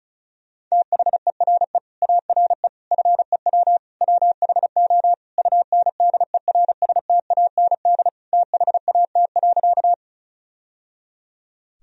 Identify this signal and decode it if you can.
Morse code, heard as a keyed tone, about 36 wpm, 700 Hz. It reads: THERE ARE FEW WHO UNDERSTAND THAT.
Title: Sentences from Top 1000 Words - Review Part 1 - 35wpm